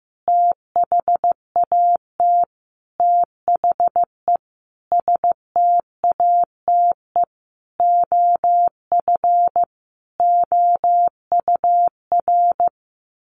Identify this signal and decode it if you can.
Morse code, heard as a keyed tone, about 15 wpm, 700 Hz. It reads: THAT THE STATE OF OUR
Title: Sentences from Top 1400 Words 15wpm